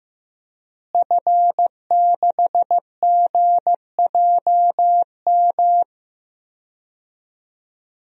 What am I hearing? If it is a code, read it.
Morse code, heard as a keyed tone, about 16 wpm, 700 Hz. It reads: F6GJM
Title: DXCC-100 Call Signs - 5 Characters - Repeated 3x - 15wpm